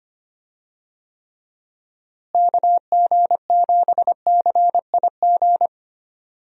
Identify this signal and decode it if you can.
Morse code, heard as a keyed tone, about 25 wpm, 700 Hz. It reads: KG7CIG
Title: USA Call Signs - 6 Characters - Repeated 2x - 25wpm